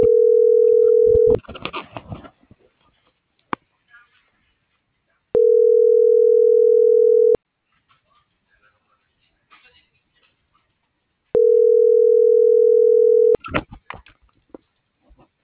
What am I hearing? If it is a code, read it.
Ringback tone (North America), 440 + 480 Hz.